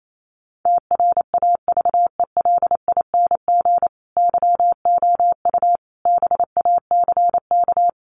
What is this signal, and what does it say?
Morse code, heard as a keyed tone, about 28 wpm, 700 Hz. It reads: TRAVELING YOU BACK
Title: A Sound Of Thunder 28wpm